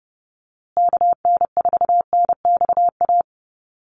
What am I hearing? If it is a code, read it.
Morse code, heard as a keyed tone, about 30 wpm, 700 Hz. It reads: KN4NXA
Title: USA Call Signs Repeated 3x 30wpm